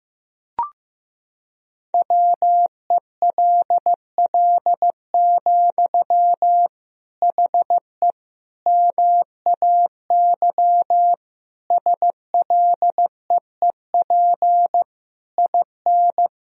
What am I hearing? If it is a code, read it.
Morse code, heard as a keyed tone, about 16 wpm, 700 Hz. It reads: WELL, HE MAY SLEEP IN
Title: Sentences from Top 1000 Words - Review Part 4 - 15wpm